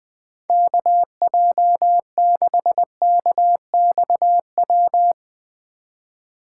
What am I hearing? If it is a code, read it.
Morse code, heard as a keyed tone, about 20 wpm, 700 Hz. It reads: KJ6KXW